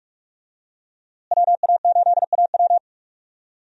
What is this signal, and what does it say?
Morse code, heard as a keyed tone, about 45 wpm, 700 Hz. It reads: WA8AW
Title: USA Call Signs - 5 Characters - Repeated 2x - 45wpm